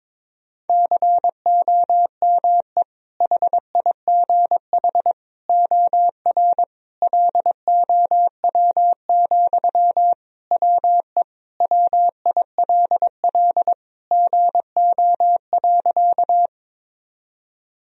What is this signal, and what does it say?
Morse code, heard as a keyed tone, about 22 wpm, 700 Hz. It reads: COME HIGH OR LOW, WE WILL GO.
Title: Sentences from Top 1000 Words - Review Part 1 - 22wpm